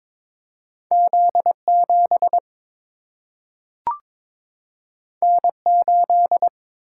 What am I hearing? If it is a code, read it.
Morse code, heard as a keyed tone, about 22 wpm, 700 Hz. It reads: Z 7 N 8
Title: Single-Letter Single-Number 22wpm